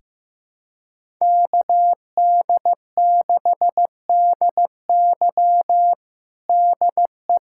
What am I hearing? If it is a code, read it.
Morse code, heard as a keyed tone, about 15 wpm, 700 Hz. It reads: KD6DY DE
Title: USA Callsign DE Callsign - 5 Characters - 15wpm